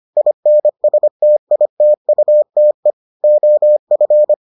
Morse code, heard as a keyed tone, about 25 wpm, 600 Hz. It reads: INSTITUTE OF